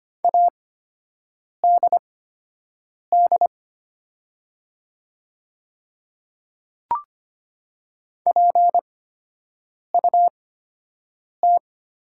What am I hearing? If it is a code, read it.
Morse code, heard as a keyed tone, about 25 wpm, spaced as for 7 wpm, 700 Hz. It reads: ADD PUT
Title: 8x Character Spacing - Top 300 Words - 25wpm